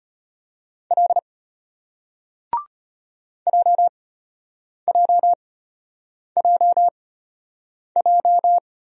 Morse code, heard as a keyed tone, about 34 wpm, spaced as for 8 wpm, 700 Hz. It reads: L JJJJ